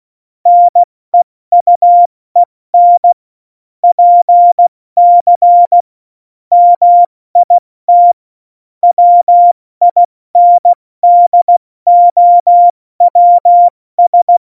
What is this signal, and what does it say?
Morse code, heard as a keyed tone, about 16 wpm, 700 Hz. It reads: NEUEN PC MIT WINDOWS
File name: Morsekurs / Witze / Witz 15. Januar 2019, 70 F80